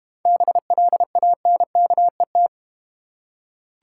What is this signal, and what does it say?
Morse code, heard as a keyed tone, about 31 wpm, 700 Hz. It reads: BLANKET
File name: Easy Words - Speed-Racing with Hints - 25wpm